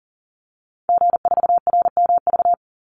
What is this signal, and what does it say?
Morse code, heard as a keyed tone, about 40 wpm, 700 Hz. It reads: G4RMV
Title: Call Signs Repeated 3x - 40wpm